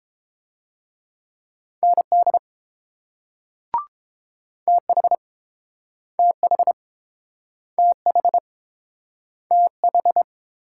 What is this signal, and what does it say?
Morse code, heard as a keyed tone, about 30 wpm, 700 Hz. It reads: ND TH TH TH TH